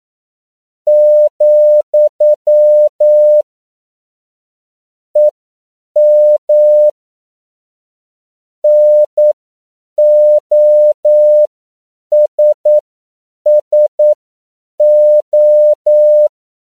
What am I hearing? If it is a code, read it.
Morse code, heard as a keyed tone, about 9 wpm, spaced as for 7 wpm, 600 Hz. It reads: , EM NOSSO
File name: CW / 10PPM